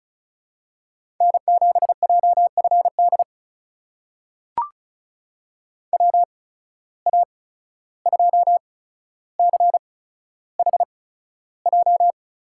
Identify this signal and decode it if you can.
Morse code, heard as a keyed tone, about 36 wpm, spaced as for 14 wpm, 700 Hz. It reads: N7JFD WA2CHJ
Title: USA Call Signs - 8x Character Spacing - 35wpm